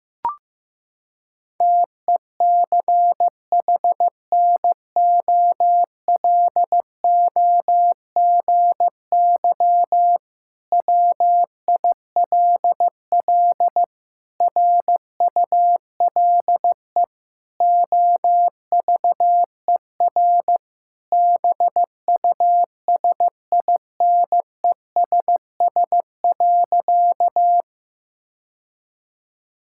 Morse code, heard as a keyed tone, about 15 wpm, 700 Hz. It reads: TECHNOLOGY WILL RULE OVER BUSINESS.
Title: Sentences from Top 1000 Words - Review Part 5 - 15wpm